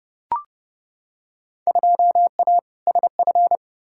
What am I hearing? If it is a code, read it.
Morse code, heard as a keyed tone, about 32 wpm, 700 Hz. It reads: 2A SF
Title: Field Day Exchanges - 30wpm